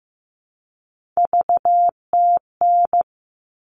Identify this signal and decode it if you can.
Morse code, heard as a keyed tone, about 15 wpm, 700 Hz. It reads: VTN
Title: All Cut Numbers - 3-Digit 15wpm